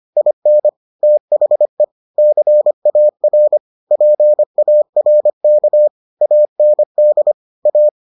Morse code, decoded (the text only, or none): IN THE CAR PARK AND A